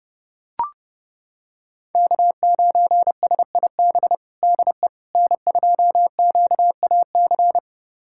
Morse code, heard as a keyed tone, about 30 wpm, 700 Hz. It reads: K9SIB DE N2QAC
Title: USA Callsign DE Callsign - 5 Characters - 30wpm